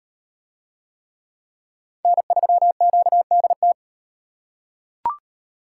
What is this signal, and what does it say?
Morse code, heard as a keyed tone, about 35 wpm, 700 Hz. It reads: N3QDT